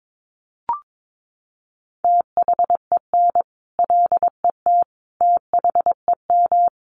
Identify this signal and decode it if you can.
Morse code, heard as a keyed tone, about 22 wpm, 700 Hz. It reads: THEN LET THEM
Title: Sentences from Top 1400 Words 22wpm